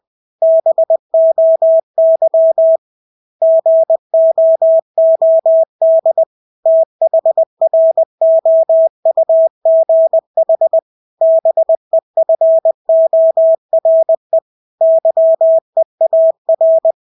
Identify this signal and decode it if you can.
Morse code, heard as a keyed tone, about 20 wpm, 650 Hz. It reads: BOY GOOD THROUGH BEFORE YEAR
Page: Engelske ord 20 wpm